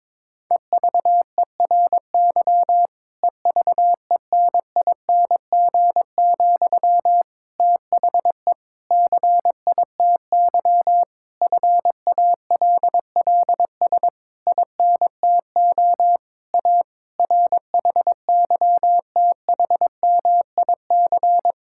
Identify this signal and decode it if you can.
Morse code, heard as a keyed tone, about 22 wpm, 700 Hz. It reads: EVERY EVENING, THE CITY FALLS INTO A RHYTHMIC
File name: Difficult Sentences with Hints - 22wpm